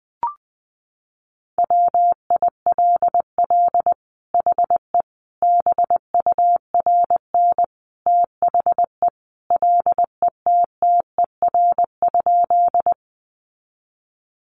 Morse code, heard as a keyed tone, about 20 wpm, 700 Hz. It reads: WILL HE BURN THE LETTER?
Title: Sentences from Top 1300 Words 20wpm